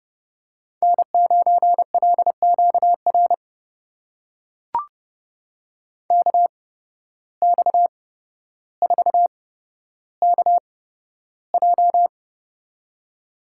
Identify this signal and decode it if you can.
Morse code, heard as a keyed tone, about 29 wpm, spaced as for 13 wpm, 700 Hz. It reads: N9LQR KX4KJ
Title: USA Call Signs - 5 Characters - 8x Character Spacing - 30wpm